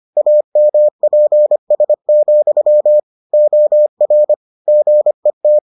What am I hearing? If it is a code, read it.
Morse code, heard as a keyed tone, about 25 wpm, 600 Hz. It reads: AMPS, OR GET